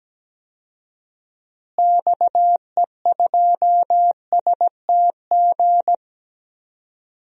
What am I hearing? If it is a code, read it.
Morse code, heard as a keyed tone, about 17 wpm, 700 Hz. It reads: XE2STG
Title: DXCC-100 Call Signs - 6 Characters - Repeated 2x - 17wpm